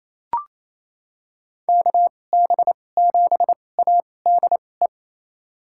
Morse code, heard as a keyed tone, about 28 wpm, spaced as for 20 wpm, 700 Hz. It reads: KB7ADE
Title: USA Call Signs - 6 Characters - 2x Character Spacing - 28wpm